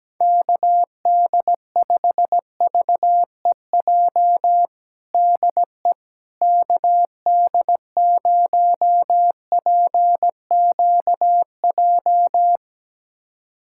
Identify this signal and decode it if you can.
Morse code, heard as a keyed tone, about 17 wpm, 700 Hz. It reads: KD5VEJ DE KD0PQJ
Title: USA Callsign DE Callsign - 6 Characters - 17wpm